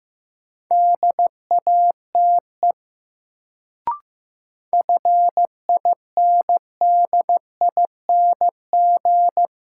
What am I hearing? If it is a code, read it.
Morse code, heard as a keyed tone, about 15 wpm, 700 Hz. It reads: DATE FINDING
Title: Sentences from Top 1100 Words - New Word Drill Alt - 15wpm